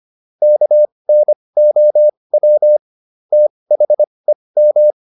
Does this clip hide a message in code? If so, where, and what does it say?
Yes: Morse code from 0.4 to 4.9 s, heard as a keyed tone, about 25 wpm, spaced as for 20 wpm, 600 Hz. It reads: KNOW THEM